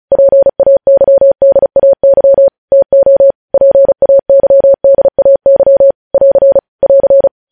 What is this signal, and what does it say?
Morse code, heard as a keyed tone, about 36 wpm, 550 Hz. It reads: PAYDAY TO PAYDAY <AR> <AR>
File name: podcast / News headlines-1x-35x35